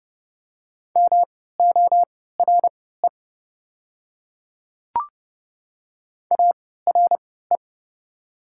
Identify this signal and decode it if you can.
Morse code, heard as a keyed tone, about 31 wpm, spaced as for 17 wpm, 700 Hz. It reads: MORE ARE
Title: 3x Character Spacing - Top 100 Words - 30wpm